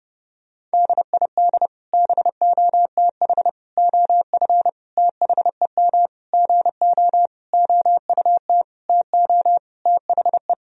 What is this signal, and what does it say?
Morse code, heard as a keyed tone, about 30 wpm, 700 Hz. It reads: DID BOTH OF THEM GO OUT TO THE
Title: Sentences from Top 1000 Words - Review Part 2 - 30wpm